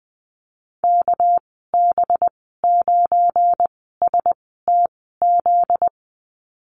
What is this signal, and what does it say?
Morse code, heard as a keyed tone, about 20 wpm, spaced as for 14 wpm, 700 Hz. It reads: KB9STZ
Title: 245 Lesson 39g - Letter z - Call Signs 20wpm